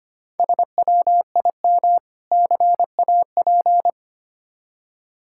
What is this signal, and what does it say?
Morse code, heard as a keyed tone, about 25 wpm, 700 Hz. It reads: SWIM CAP
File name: practice / 25wpm / Difficult Words with Hints - 25wpm